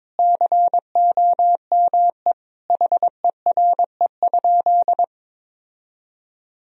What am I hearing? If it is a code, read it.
Morse code, heard as a keyed tone, about 21 wpm, 700 Hz. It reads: COME HERE?
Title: Sentences from Top 1000 Words - Review Part 4 - 22wpm